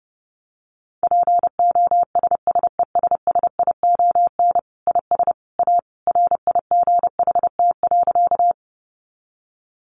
Morse code, heard as a keyed tone, about 30 wpm, 700 Hz. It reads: POSSESSION IS A RIGHT.
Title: Sentences from Top 1400 Words 30wpm